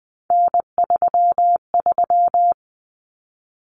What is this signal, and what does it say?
Morse code, heard as a keyed tone, about 20 wpm, 700 Hz. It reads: N33
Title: Common Cut Numbers - 3-Digit 20wpm